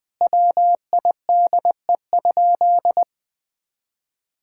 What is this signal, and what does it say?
Morse code, heard as a keyed tone, about 21 wpm, 700 Hz. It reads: WIDE?
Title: Sentences from Top 1000 Words - Review Part 6 - 20wpm